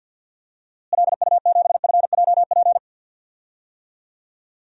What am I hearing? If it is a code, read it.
Morse code, heard as a keyed tone, about 52 wpm, 700 Hz. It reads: LU7FJP
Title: DXCC-100 Call Signs - 6 Characters - Repeated 3x - 50wpm